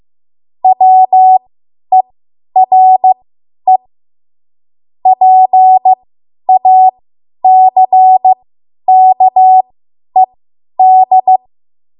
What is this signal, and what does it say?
Morse code, heard as a keyed tone, about 15 wpm, spaced as for 10 wpm, 750 Hz. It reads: WERE PACKED
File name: Morse / 10WPM